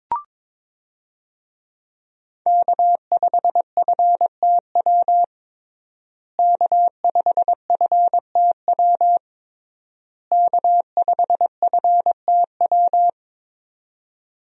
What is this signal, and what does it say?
Morse code, heard as a keyed tone, about 22 wpm, 700 Hz. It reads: K5FTW K5FTW K5FTW
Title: USA Call Signs - 5 Characters - Repeated 3x - 22wpm